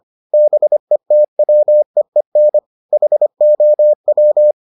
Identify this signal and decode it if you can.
Morse code, heard as a keyed tone, about 25 wpm, 600 Hz. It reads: BETWEEN HOW